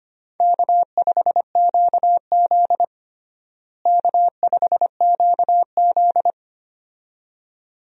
Morse code, heard as a keyed tone, about 25 wpm, 700 Hz. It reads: K5QZ K5QZ
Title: USA Call Signs - 4 Characters - Repeated 2x - 25wpm